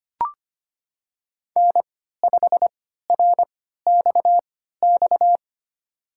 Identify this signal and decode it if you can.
Morse code, heard as a keyed tone, about 25 wpm, spaced as for 14 wpm, 700 Hz. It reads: N5RXX
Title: USA Call Signs - 5 Characters - 3x Character Spacing - 25wpm